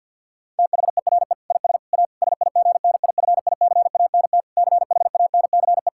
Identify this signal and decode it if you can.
Morse code, heard as a keyed tone, about 51 wpm, 700 Hz. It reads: THERE IS A SIGNIFICANT CHANCE
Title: Sentences from Top 1000 Words - Review Part 6 - 50wpm